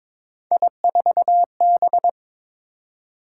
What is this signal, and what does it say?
Morse code, heard as a keyed tone, about 22 wpm, 700 Hz. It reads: I4B
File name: practice / 22wpm / Three Letter-Number Combinations - 22wpm